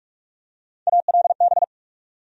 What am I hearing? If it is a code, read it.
Morse code, heard as a keyed tone, about 46 wpm, 700 Hz. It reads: ALB
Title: USA Call Sign Postfixes - 3 Letters 45wpm